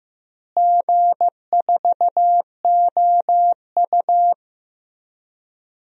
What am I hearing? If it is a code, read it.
Morse code, heard as a keyed tone, about 15 wpm, 700 Hz. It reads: G4OU
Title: DXCC-100 Call Signs - 4 Characters - Repeated 3x - 15wpm